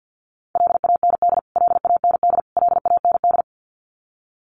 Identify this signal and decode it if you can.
Morse code, heard as a keyed tone, about 50 wpm, 700 Hz. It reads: LAND LAND LAND
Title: Top 500 Words Repeated 50wpm